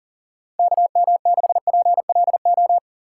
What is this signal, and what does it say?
Morse code, heard as a keyed tone, about 40 wpm, 700 Hz. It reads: KM6PLO
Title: USA Call Signs - 3x Character Spacing - 40wpm